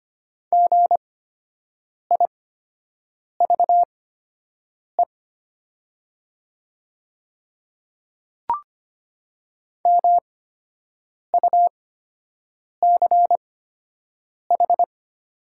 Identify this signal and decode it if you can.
Morse code, heard as a keyed tone, about 25 wpm, spaced as for 7 wpm, 700 Hz. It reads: GIVE MUCH
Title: 8x Character Spacing - Top 300 Words - 25wpm